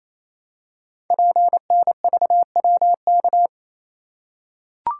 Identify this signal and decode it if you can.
Morse code, heard as a keyed tone, about 27 wpm, 700 Hz. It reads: PNVWK
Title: Code Groups - 5 Letters - 28wpm